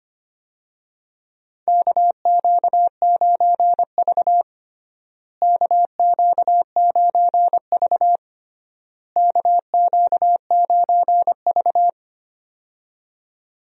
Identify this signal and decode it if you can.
Morse code, heard as a keyed tone, about 25 wpm, 700 Hz. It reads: KQ9V KQ9V KQ9V
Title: USA Call Signs - 4 Characters - Repeated 3x - 25wpm